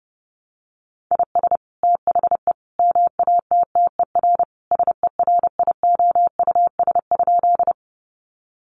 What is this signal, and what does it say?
Morse code, heard as a keyed tone, about 30 wpm, 700 Hz. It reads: IS THE MATTER SERIOUS?
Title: Sentences from Top 1100 Words 30wpm